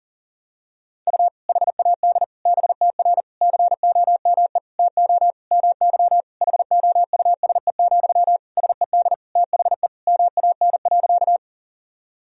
Morse code, heard as a keyed tone, about 40 wpm, 700 Hz. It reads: U HAD BTR COME TO MY HOUSE, SED THE MAN.
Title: Sentences from Top 400 Words and Abbreviations 40wpm